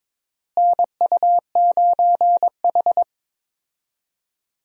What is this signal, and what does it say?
Morse code, heard as a keyed tone, about 22 wpm, 700 Hz. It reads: NU9H
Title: USA Call Signs - 4 Characters - Repeated 3x - 22wpm